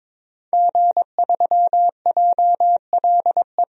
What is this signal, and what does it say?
Morse code, heard as a keyed tone, about 22 wpm, 700 Hz. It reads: G3JLE
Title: DXCC-100 Call Signs - 5 Characters - 22wpm